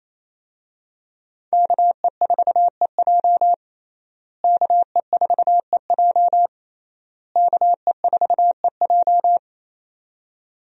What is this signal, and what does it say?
Morse code, heard as a keyed tone, about 28 wpm, 700 Hz. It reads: KE4EJ KE4EJ KE4EJ
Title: USA Call Signs - 5 Characters - Repeated 3x - 28wpm